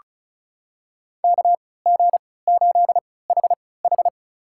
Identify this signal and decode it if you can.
Morse code, heard as a keyed tone, about 34 wpm, spaced as for 20 wpm, 700 Hz. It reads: KG8HH
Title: USA Call Signs - 5 Characters - 3x Character Spacing - 35wpm